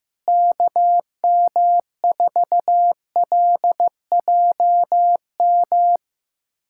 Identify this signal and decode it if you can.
Morse code, heard as a keyed tone, about 15 wpm, 700 Hz. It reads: KM4LJM
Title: USA Callsign DE Callsign - 6 Characters - 15wpm